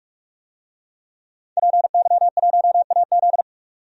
Morse code, heard as a keyed tone, about 47 wpm, 700 Hz. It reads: PY1AZ
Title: DXCC-100 Call Signs - 5 Characters - 45wpm